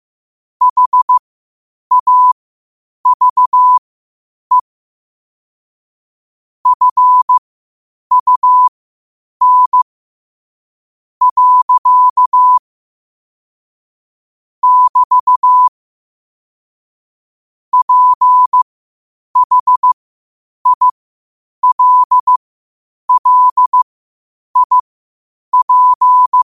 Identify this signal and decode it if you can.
Morse code, heard as a keyed tone, about 15 wpm, spaced as for 8 wpm, 1000 Hz. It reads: HAVE FUN. <BT> PHILLIP
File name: qotdfiles / 8wpm